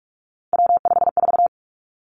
Morse code, heard as a keyed tone, about 46 wpm, 700 Hz. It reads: W54
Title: Three Challenging Letter-Number Combinations 45wpm